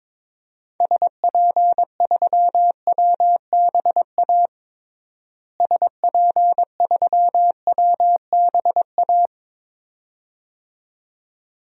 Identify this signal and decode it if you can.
Morse code, heard as a keyed tone, about 22 wpm, 700 Hz. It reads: SP3WBA SP3WBA
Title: DXCC-100 Call Signs - 6 Characters - Repeated 2x - 22wpm